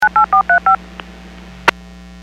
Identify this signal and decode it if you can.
DTMF, heard as telephone touch tones, tones of about 110 ms. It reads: 98435